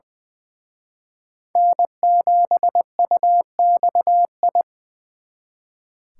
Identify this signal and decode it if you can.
Morse code, heard as a keyed tone, about 24 wpm, 700 Hz. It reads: N7UXI
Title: Speed Racing - USA Call Signs 20wpm